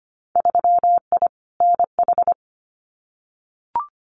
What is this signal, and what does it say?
Morse code, heard as a keyed tone, about 25 wpm, 700 Hz. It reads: 3I NH
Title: Winter Field Day Exchanges Encore 25wpm